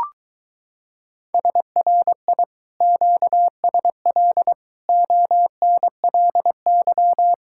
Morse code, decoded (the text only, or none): SRI QSL ONLY